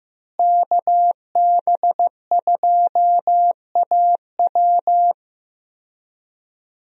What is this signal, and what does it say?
Morse code, heard as a keyed tone, about 15 wpm, 700 Hz. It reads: KB2AW
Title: USA Call Signs - 5 Characters - Repeated 3x - 15wpm